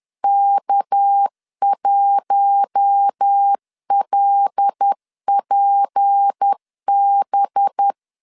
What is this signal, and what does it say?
Morse code, heard as a keyed tone, about 10 wpm, 800 Hz. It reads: K1LPB
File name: CW / DET / 1 6 1500C S B 10wpm 4137